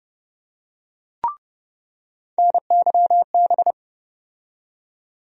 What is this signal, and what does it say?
Morse code, heard as a keyed tone, about 30 wpm, 700 Hz. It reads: NYB
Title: USA Call Sign Postfixes - 3 Letters 30wpm